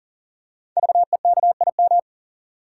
Morse code, heard as a keyed tone, about 39 wpm, 700 Hz. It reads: VEKIM
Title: Code Groups - 5 Letters - 40wpm